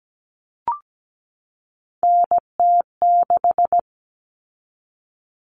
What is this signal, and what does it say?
Morse code, heard as a keyed tone, about 17 wpm, 700 Hz. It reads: NT6
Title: All Cut Numbers - 3-Digit 17wpm